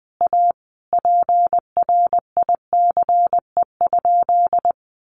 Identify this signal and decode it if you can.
Morse code, heard as a keyed tone, about 20 wpm, 700 Hz. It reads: A PRICE?
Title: Sentences from Top 1400 Words 20wpm